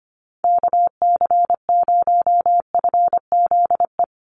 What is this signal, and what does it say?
Morse code, heard as a keyed tone, about 25 wpm, 700 Hz. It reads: KC0FZE
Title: USA Call Signs 25wpm